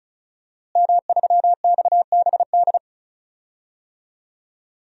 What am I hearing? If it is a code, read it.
Morse code, heard as a keyed tone, about 36 wpm, 700 Hz. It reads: M3XBD
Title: DXCC-100 Call Signs - 5 Characters - Repeated 3x - 35wpm